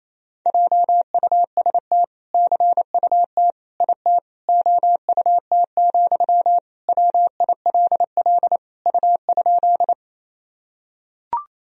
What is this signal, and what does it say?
Morse code, heard as a keyed tone, about 28 wpm, 700 Hz. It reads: JUST CUT IT OUT, WILL U?
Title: Sentences from Top 300 Words and Abbreviations 28wpm